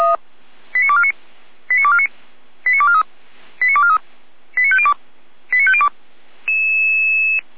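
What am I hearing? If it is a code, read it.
Five-tone selective call (ZVEI), heard as quick runs of five tones, tones of about 70 ms. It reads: 89249 89249 89243 89243 89502 89502
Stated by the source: ZVEI five-tone selective call received by SDR, server of Feuerwehr Orlamünde fire brigade.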